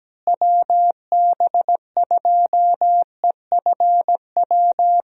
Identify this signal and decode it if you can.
Morse code, heard as a keyed tone, about 17 wpm, 700 Hz. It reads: WB2EFW
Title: USA Call Signs - 6 Characters - Repeated 3x - 17wpm